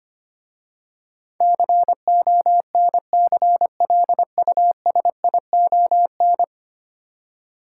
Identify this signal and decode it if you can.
Morse code, heard as a keyed tone, about 25 wpm, 700 Hz. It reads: CONCLUSION
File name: 10 Letter Words - First 3 Letters Given - 25wpm